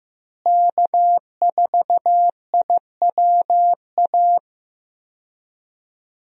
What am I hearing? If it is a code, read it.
Morse code, heard as a keyed tone, about 15 wpm, 700 Hz. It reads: K4IWA
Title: USA Call Signs - 5 Characters - Repeated 3x - 15wpm